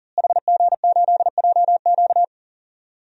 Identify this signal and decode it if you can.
Morse code, heard as a keyed tone, about 40 wpm, 700 Hz. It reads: HG8JQ
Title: DXCC-100 Call Signs - 5 Characters - Repeated 3x - 40wpm